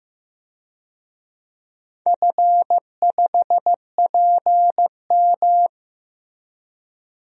Morse code, heard as a keyed tone, about 15 wpm, 700 Hz. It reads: F5PM
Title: DXCC-100 Call Signs - 4 Characters - Repeated 3x - 15wpm